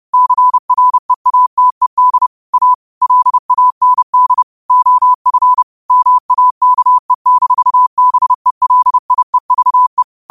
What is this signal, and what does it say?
Morse code, heard as a keyed tone, about 30 wpm, 1000 Hz. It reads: CREATED A LAND OF MAKE-BELIEVE